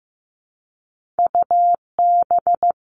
Morse code, heard as a keyed tone, about 15 wpm, 700 Hz. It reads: UB
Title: All Cut Numbers - 2-Digit 15wpm